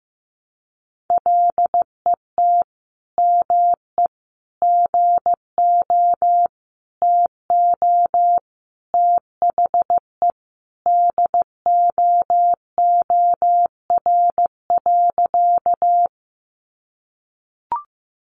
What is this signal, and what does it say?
Morse code, heard as a keyed tone, about 15 wpm, 700 Hz. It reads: LET ME GO TO THE DOOR.
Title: Sentences from Top 300 Words 15wpm